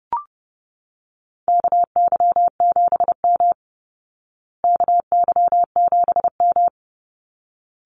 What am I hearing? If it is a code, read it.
Morse code, heard as a keyed tone, about 30 wpm, 700 Hz. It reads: KY7M KY7M
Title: Call Signs Repeated 3x - 30wpm